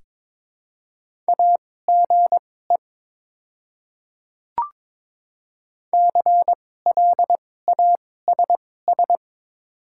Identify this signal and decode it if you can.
Morse code, heard as a keyed tone, about 22 wpm, spaced as for 16 wpm, 700 Hz. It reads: AGE CLASS
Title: 2x Character Spacing - Top 500 Words - 22wpm